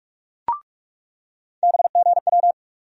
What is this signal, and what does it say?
Morse code, heard as a keyed tone, about 46 wpm, 700 Hz. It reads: BGW
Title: USA Call Sign Postfixes - 3 Letters 45wpm